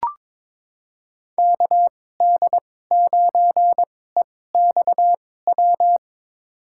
Morse code, heard as a keyed tone, about 22 wpm, spaced as for 16 wpm, 700 Hz. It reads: KD9EXW
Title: USA Call Signs - 6 Characters - 2x Character Spacing - 22wpm